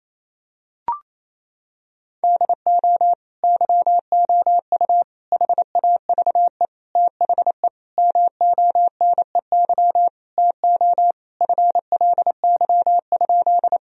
Morse code, heard as a keyed tone, about 28 wpm, 700 Hz. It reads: DO YOU HAVE THE MONEY TO FLY?
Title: Sentences from Top 1000 Words - Review Part 2 - 28wpm